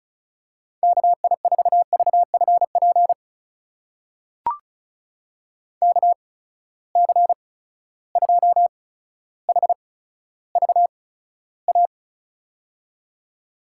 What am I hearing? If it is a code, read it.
Morse code, heard as a keyed tone, about 36 wpm, spaced as for 15 wpm, 700 Hz. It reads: KI4VFP KC2HVA
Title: USA Call Signs - 6 Characters - 8x Character Spacing - 35wpm